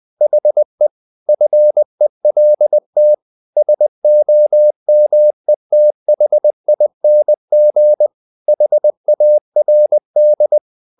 Morse code, heard as a keyed tone, about 20 wpm, 600 Hz. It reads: HE FELT SOMETHING HARD
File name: the old man and the sea / the old man and the sea part 1 20wpm